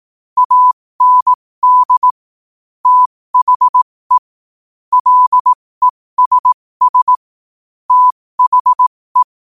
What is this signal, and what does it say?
Morse code, heard as a keyed tone, about 18 wpm, spaced as for 15 wpm, 1000 Hz. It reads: AND THE LESS THE
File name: qotdfiles / 15wpm